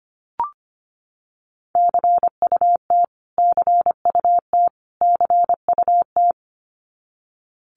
Morse code, heard as a keyed tone, about 25 wpm, 700 Hz. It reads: CUT CUT CUT
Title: Top 500 Words Repeated 25wpm